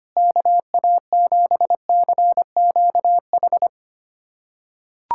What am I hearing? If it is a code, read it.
Morse code, heard as a keyed tone, about 25 wpm, 700 Hz. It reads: KA7CQH
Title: USA Callsign DE Callsign 25wpm